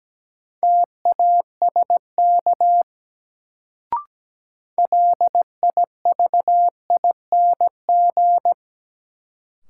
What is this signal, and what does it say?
Morse code, heard as a keyed tone, about 17 wpm, 700 Hz. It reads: TASK LIVING
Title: Sentences from Top 1100 Words - New Word Drill Alt - 17wpm